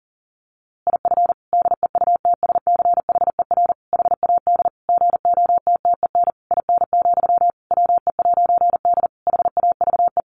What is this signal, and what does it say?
Morse code, heard as a keyed tone, about 40 wpm, 700 Hz. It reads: IF DEUTSCHER HAD GOTTEN IN, WE'D HAVE
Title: A Sound Of Thunder 40wpm